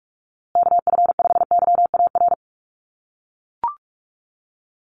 Morse code, heard as a keyed tone, about 44 wpm, 700 Hz. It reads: KF5YAR
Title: USA Call Signs Repeated 3x 45wpm